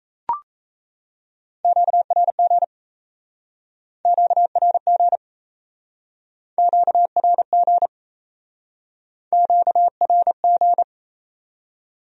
Morse code, heard as a keyed tone, about 37 wpm, 700 Hz. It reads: QRG QRG QRG QRG